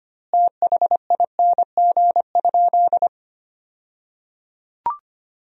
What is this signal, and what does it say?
Morse code, heard as a keyed tone, about 25 wpm, 700 Hz. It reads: THING?
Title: 2x Word-Spacing - Sentences from Top 200 Words - 25wpm